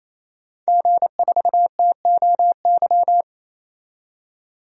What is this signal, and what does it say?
Morse code, heard as a keyed tone, about 28 wpm, 700 Hz. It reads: G4TOY
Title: DXCC-100 Call Signs - 5 Characters - 28wpm